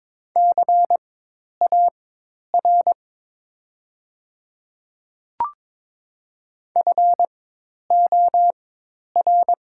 Morse code, heard as a keyed tone, about 22 wpm, spaced as for 10 wpm, 700 Hz. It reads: CAR FOR
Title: 4x Character Spacing - Top 300 Words - 22wpm